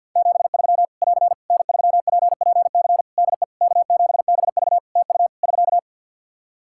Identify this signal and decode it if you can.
Morse code, heard as a keyed tone, about 50 wpm, 700 Hz. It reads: 73 <AR> N3PPC DE K7BV TU <SK>
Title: Practice QSO - Exchange 4 - Ending - 50wpm